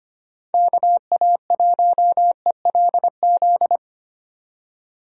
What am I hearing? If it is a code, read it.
Morse code, heard as a keyed tone, about 25 wpm, 700 Hz. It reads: KA1ELZ
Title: USA Call Signs - 6 Characters - 25wpm